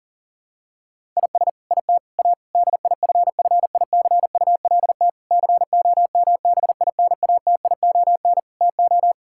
Morse code, heard as a keyed tone, about 40 wpm, 700 Hz. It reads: IS IT A DIFFICULT COMBINATION TO
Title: Sentences from Top 1000 Words - Review Part 6 - 40wpm